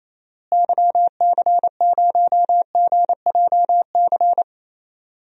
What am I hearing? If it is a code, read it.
Morse code, heard as a keyed tone, about 27 wpm, 700 Hz. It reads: YC0GJC
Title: DXCC-100 Call Signs - 6 Characters - Repeated 3x - 28wpm